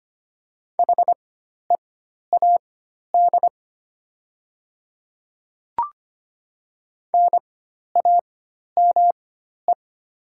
Morse code, heard as a keyed tone, about 25 wpm, spaced as for 12 wpm, 690 Hz. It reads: HEAD NAME
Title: 4x Character Spacing - Top 300 Words - 25wpm